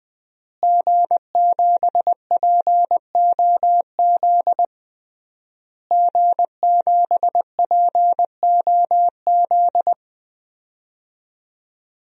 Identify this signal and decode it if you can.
Morse code, heard as a keyed tone, about 20 wpm, 700 Hz. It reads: G7POZ G7POZ
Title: DXCC-100 Call Signs - 5 Characters - Repeated 3x - 20wpm